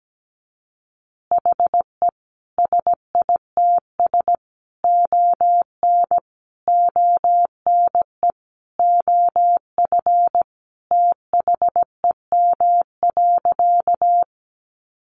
Morse code, heard as a keyed tone, about 17 wpm, 700 Hz. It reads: HE SITS ON ONE OF THEM.
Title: Additional Easy Phrases 17wpm